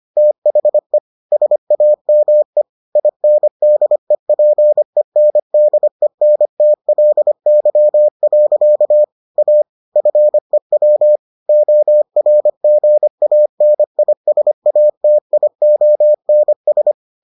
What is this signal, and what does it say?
Morse code, heard as a keyed tone, about 25 wpm, 600 Hz. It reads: THE SAME INDEPENDENTLY. A FEW ORGANISATIONS